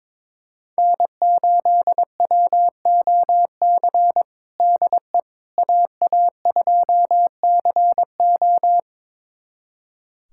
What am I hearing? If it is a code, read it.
Morse code, heard as a keyed tone, about 22 wpm, 700 Hz. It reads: N8WOC DE AA2CO
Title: USA Callsign DE Callsign - 5 Characters - 22wpm